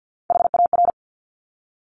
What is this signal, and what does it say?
Morse code, heard as a keyed tone, about 50 wpm, 700 Hz. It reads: HAR
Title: Top Three Letter Combinations in Words 50wpm